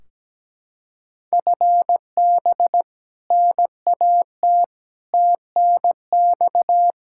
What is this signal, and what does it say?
Morse code, heard as a keyed tone, about 17 wpm, 700 Hz. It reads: FB NAT TNX
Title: Practice QSO - Exchange 3 - More Info - 17wpm